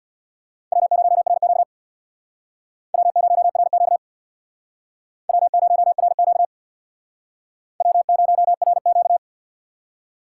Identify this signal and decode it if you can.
Morse code, heard as a keyed tone, about 61 wpm, 700 Hz. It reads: W0RQ W0RQ W0RQ W0RQ
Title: Speed Racing - USA 1x2 Call Signs 50wpm